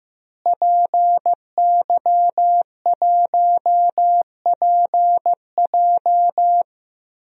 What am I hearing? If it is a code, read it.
Morse code, heard as a keyed tone, about 16 wpm, 700 Hz. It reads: PY1PJ
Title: DXCC-100 Call Signs - 5 Characters - Repeated 3x - 15wpm